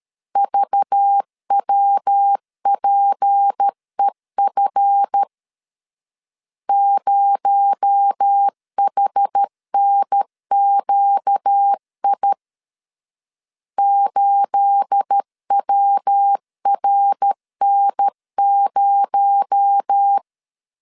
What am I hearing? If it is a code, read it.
Morse code, heard as a keyed tone, about 13 wpm, 800 Hz. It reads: VWPEF 0HNQI 8WRN0
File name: CW / DET / 1 9 1500C S B 12wpm 3819